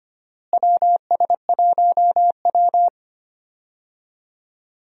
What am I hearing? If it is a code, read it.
Morse code, heard as a keyed tone, about 25 wpm, 700 Hz. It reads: WS1W
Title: USA Call Signs - 4 Characters - Repeated 2x - 25wpm